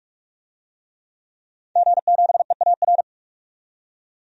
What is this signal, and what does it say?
Morse code, heard as a keyed tone, about 46 wpm, 700 Hz. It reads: G7EAR
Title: DXCC-100 Call Signs - 5 Characters - 45wpm